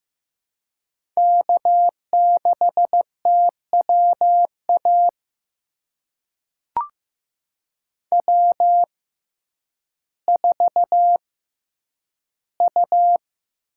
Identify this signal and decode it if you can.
Morse code, heard as a keyed tone, about 15 wpm, spaced as for 9 wpm, 700 Hz. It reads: K6TWA W4U
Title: USA Call Signs - 5 Characters - 6x Character Spacing - 15wpm